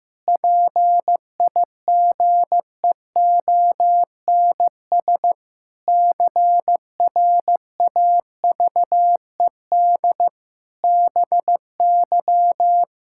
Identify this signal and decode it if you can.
Morse code, heard as a keyed tone, about 15 wpm, 700 Hz. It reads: PIGEONS CRAVED BY
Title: Pangram Challenge v2 - 15wpm